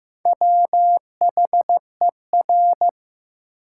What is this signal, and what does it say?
Morse code, heard as a keyed tone, about 15 wpm, 700 Hz. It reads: WHER
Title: Sentences from Top 1500 Words - New Word Drill Alt - 15wpm